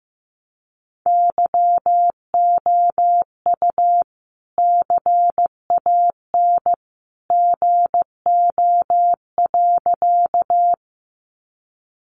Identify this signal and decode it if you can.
Morse code, heard as a keyed tone, about 15 wpm, 700 Hz. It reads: YOU CAN GO.
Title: Sentences from Top 100 Words 15wpm